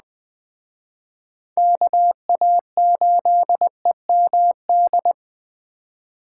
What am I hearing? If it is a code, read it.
Morse code, heard as a keyed tone, about 21 wpm, 700 Hz. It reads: KA8EMD